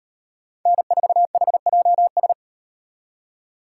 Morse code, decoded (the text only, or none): N4HJS